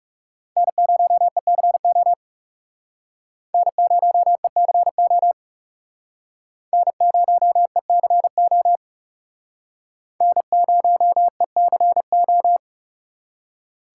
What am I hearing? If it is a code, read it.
Morse code, heard as a keyed tone, about 38 wpm, 700 Hz. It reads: N0ECO N0ECO N0ECO N0ECO